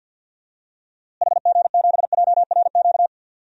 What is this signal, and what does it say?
Morse code, heard as a keyed tone, about 53 wpm, 700 Hz. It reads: HG7JRQ
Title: DXCC-100 Call Signs - 6 Characters - Repeated 2x - 50wpm